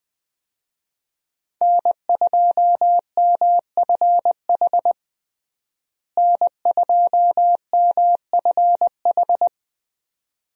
Morse code, heard as a keyed tone, about 21 wpm, 700 Hz. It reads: N2MFH N2MFH
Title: USA Call Signs - 5 Characters - Repeated 2x - 20wpm